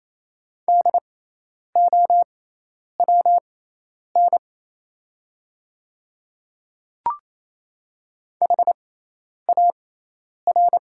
Morse code, heard as a keyed tone, about 28 wpm, spaced as for 10 wpm, 700 Hz. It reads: DOWN HAR
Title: 6x Character Spacing - Top 300 Words - 28wpm